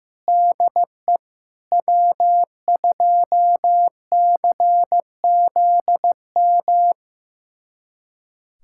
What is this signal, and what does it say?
Morse code, heard as a keyed tone, about 15 wpm, 700 Hz. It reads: DE W2CZM
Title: USA Callsign DE Callsign 15wpm